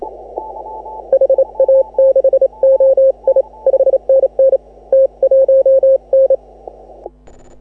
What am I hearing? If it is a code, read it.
Morse code, heard as a keyed tone, about 26 wpm, 530 Hz. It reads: HA6OI 5NN T1N
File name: AUDIO2018 / HA6OI CW